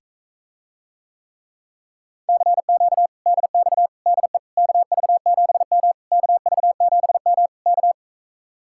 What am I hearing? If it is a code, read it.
Morse code, heard as a keyed tone, about 46 wpm, 700 Hz. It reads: CQ DX DE KV7M KV7M K